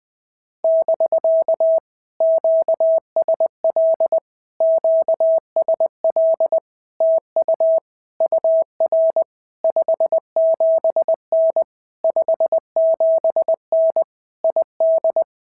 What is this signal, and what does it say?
Morse code, heard as a keyed tone, about 19 wpm, 650 Hz. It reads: <BK> QSL QSL TU UR 57N 57N ID
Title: POTA QSO Practice - 20wpm